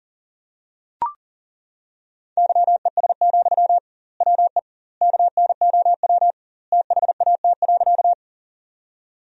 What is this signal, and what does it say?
Morse code, heard as a keyed tone, about 40 wpm, 700 Hz. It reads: YES, WE KNOW THAT.
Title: 2x Word-Spacing - Sentences from Top 500 Words - 40wpm